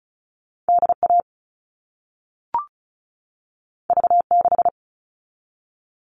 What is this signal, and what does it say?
Morse code, heard as a keyed tone, about 36 wpm, 700 Hz. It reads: DA V6
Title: All Cut Numbers - 2-Digit 35wpm